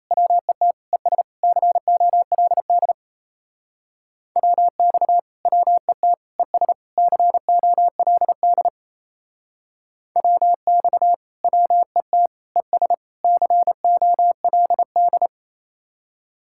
Morse code, heard as a keyed tone, about 34 wpm, 700 Hz. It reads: WET ES COLD WX WET ES COLD WX WET ES COLD
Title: Speed-Racing Common QSO Phrases - 28wpm